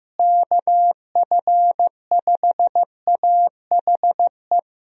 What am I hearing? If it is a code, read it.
Morse code, heard as a keyed tone, about 15 wpm, 700 Hz. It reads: KF5AHE
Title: USA Call Signs - 6 Characters - Repeated 2x - 15wpm